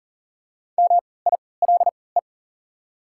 Morse code, heard as a keyed tone, about 40 wpm, spaced as for 23 wpm, 690 Hz. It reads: MILE
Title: 3x Character Spacing - Top 300 Words - 40wpm